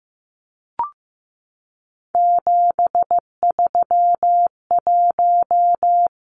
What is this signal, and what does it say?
Morse code, heard as a keyed tone, about 15 wpm, 700 Hz. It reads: 731
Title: Numbers - 3-Digit 15wpm